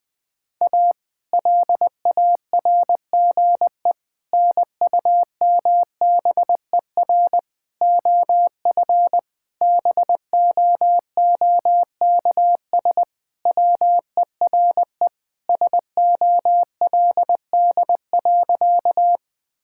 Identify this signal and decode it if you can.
Morse code, heard as a keyed tone, about 20 wpm, 700 Hz. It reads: A LARGE NUMBER OF BOOKS WERE SOLD.
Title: Sentences with 4-Word Phrases - 20wpm